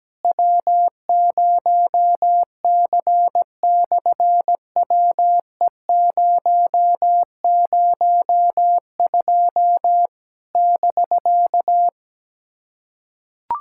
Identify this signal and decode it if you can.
Morse code, heard as a keyed tone, about 17 wpm, 700 Hz. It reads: W0C/WE002 <BK>
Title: SOTA QSO Practice 17wpm